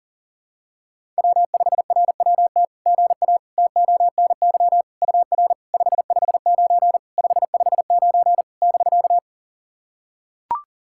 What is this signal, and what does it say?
Morse code, heard as a keyed tone, about 40 wpm, 700 Hz. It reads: W5RWT GA TONY UR 559 559 <BK>
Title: SOTA QSO Practice 40wpm